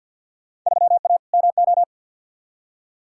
Morse code, heard as a keyed tone, about 50 wpm, 700 Hz. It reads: 3A MO
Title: Field Day Exchanges Encore - 50wpm